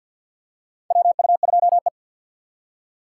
Morse code, heard as a keyed tone, about 47 wpm, 700 Hz. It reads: WU2E
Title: USA Call Signs - 4 Characters - 2x Character Spacing - 50wpm